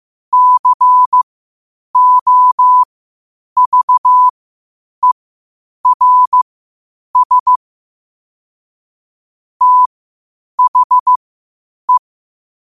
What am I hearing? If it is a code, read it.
Morse code, heard as a keyed tone, about 15 wpm, spaced as for 8 wpm, 1000 Hz. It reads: COVERS THE
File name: qotdfiles / 8wpm